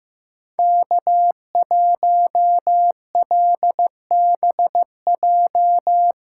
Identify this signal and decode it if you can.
Morse code, heard as a keyed tone, about 15 wpm, 700 Hz. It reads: K1LBJ
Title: USA Callsign DE Callsign - 5 Characters - 15wpm